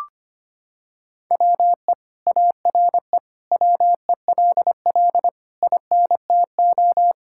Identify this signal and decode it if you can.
Morse code, heard as a keyed tone, about 25 wpm, 700 Hz. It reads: WE ARE WELL INTO
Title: Sentences from Top 1000 Words - Review Part 5 - 25wpm